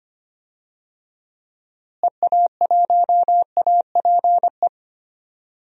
Morse code, heard as a keyed tone, about 25 wpm, 700 Hz. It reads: EA1APE
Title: DXCC-100 Call Signs - 6 Characters - Repeated 2x - 25wpm